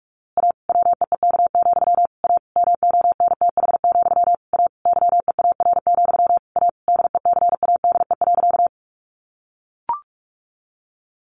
Morse code, heard as a keyed tone, about 45 wpm, 700 Hz. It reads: A WEEK, A MONTH, A YEAR, A DECADE.
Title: A Sound Of Thunder 45wpm